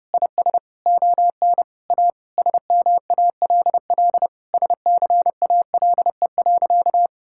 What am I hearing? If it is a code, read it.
Morse code, heard as a keyed tone, about 30 wpm, 700 Hz. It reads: IS ON A SMALL SCALE.
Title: Sentences from Top 1000 Words - Review Part 5 - 30wpm